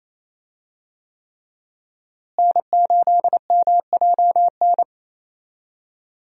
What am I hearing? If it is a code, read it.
Morse code, heard as a keyed tone, about 27 wpm, 700 Hz. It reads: N8MJN
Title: USA Call Signs - 5 Characters - Repeated 3x - 28wpm